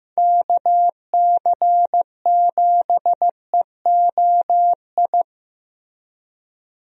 Morse code, heard as a keyed tone, about 15 wpm, 700 Hz. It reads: KC7EOI
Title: USA Call Signs - 6 Characters - Repeated 2x - 15wpm